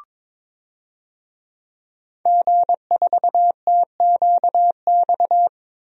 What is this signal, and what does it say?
Morse code, heard as a keyed tone, about 21 wpm, 700 Hz. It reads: G4TQX
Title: DXCC-100 Call Signs - 5 Characters - 22wpm